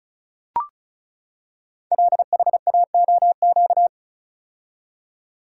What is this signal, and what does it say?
Morse code, heard as a keyed tone, about 36 wpm, 700 Hz. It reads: LHAOQ
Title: Code Groups - 5 Letters - 35wpm